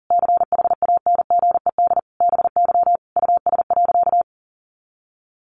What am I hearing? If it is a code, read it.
Morse code, heard as a keyed tone, about 40 wpm, 700 Hz. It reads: CHANGED BY US.
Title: A Sound Of Thunder 40wpm